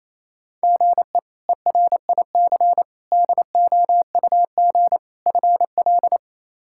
Morse code, heard as a keyed tone, about 28 wpm, 700 Hz. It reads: GE ERIC DOUG FL
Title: K1USN SST QSO Practice - 28wpm